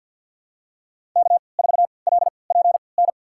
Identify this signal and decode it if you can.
Morse code, heard as a keyed tone, about 53 wpm, spaced as for 29 wpm, 700 Hz. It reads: K4LPN
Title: USA Call Signs - 5 Characters - 3x Character Spacing - 50wpm